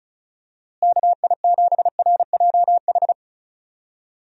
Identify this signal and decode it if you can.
Morse code, heard as a keyed tone, about 34 wpm, 700 Hz. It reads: KI7RJH
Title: USA Call Signs - 6 Characters - Repeated 3x - 35wpm